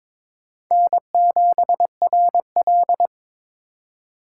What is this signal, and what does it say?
Morse code, heard as a keyed tone, about 23 wpm, 700 Hz. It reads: N7RL
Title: USA Call Signs - 4 Characters - 4x Character Spacing - 22wpm